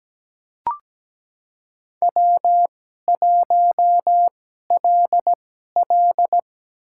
Morse code, heard as a keyed tone, about 17 wpm, spaced as for 12 wpm, 700 Hz. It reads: W1LL
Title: USA Call Signs - 4 Characters - 2x Character Spacing - 17wpm